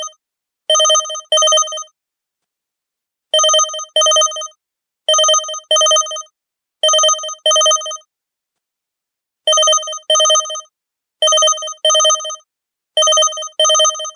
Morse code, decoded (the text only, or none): none